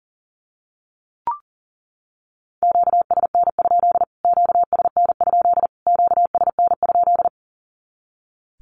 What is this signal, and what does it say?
Morse code, heard as a keyed tone, about 40 wpm, 700 Hz. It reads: QSN? QSN? QSN?
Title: Q-Codes Repeated 3x 40wpm